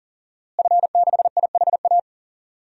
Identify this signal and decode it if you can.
Morse code, heard as a keyed tone, about 40 wpm, 700 Hz. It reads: F6IHA
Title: DXCC-100 Call Signs - 5 Characters - Repeated 3x - 40wpm